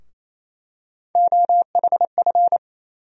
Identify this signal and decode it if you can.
Morse code, heard as a keyed tone, about 28 wpm, 700 Hz. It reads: OHF
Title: USA Call Sign Postfixes - 3 Letters 28wpm